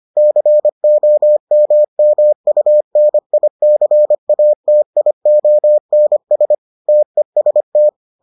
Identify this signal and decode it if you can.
Morse code, heard as a keyed tone, about 25 wpm, 600 Hz. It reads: COMMUNICATIONS TEST